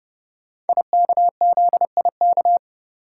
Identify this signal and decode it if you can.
Morse code, heard as a keyed tone, about 29 wpm, 700 Hz. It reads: IKZIK
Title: Code Groups - 5 Letters - 30wpm